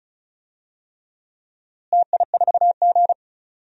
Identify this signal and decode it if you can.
Morse code, heard as a keyed tone, about 34 wpm, 700 Hz. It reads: TI4G
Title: DXCC-100 Call Signs - 4 Characters - Repeated 3x - 35wpm